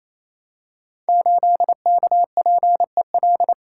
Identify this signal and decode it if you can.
Morse code, heard as a keyed tone, about 28 wpm, 700 Hz. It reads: 8KPEL
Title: Code Groups - 5 Letter:Numbers - 28wpm